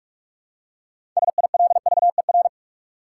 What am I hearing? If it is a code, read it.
Morse code, heard as a keyed tone, about 46 wpm, 700 Hz. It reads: SILVER
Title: Six Letter Words - 45wpm